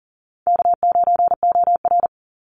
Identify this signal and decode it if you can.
Morse code, heard as a keyed tone, about 40 wpm, 700 Hz. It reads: K9OR
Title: Easy Call Signs - 40wpm